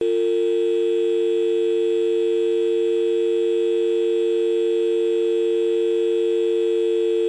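Dial tone (UK), 350 + 450 Hz.
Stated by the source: Dial-tone sound effect from the rc3 2D world.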